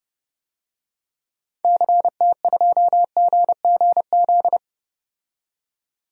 Morse code, heard as a keyed tone, about 31 wpm, 700 Hz. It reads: CT2GGZ
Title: DXCC-100 Call Signs - 6 Characters - 30wpm